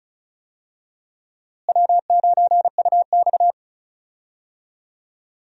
Morse code, heard as a keyed tone, about 34 wpm, 700 Hz. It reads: W9UX
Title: USA Call Signs - 4 Characters - Repeated 3x - 35wpm